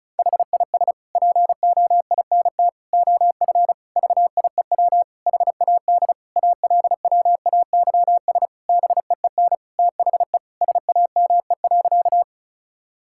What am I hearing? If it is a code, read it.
Morse code, heard as a keyed tone, about 36 wpm, 690 Hz. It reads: HIS POINT OF VIEW HAD ALWAYS BEEN THE SAME.
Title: Sentences from Top 1000 Words - Review Part 6 - 35wpm